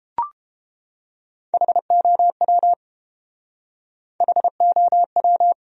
Morse code, heard as a keyed tone, about 31 wpm, 700 Hz. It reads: HOW HOW